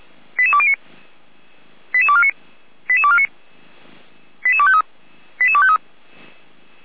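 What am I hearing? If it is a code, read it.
Five-tone selective call (ZVEI), heard as quick runs of five tones, tones of about 70 ms. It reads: 88209 88249 88249 88253 88253
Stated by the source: ZVEI five-tone call received by SDR, from feuerwehr-orlamuende.org fire brigade audio archive.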